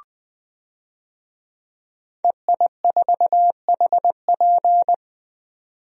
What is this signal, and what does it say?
Morse code, heard as a keyed tone, about 20 wpm, 700 Hz. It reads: EI4HP
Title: DXCC-100 Call Signs - 5 Characters - 20wpm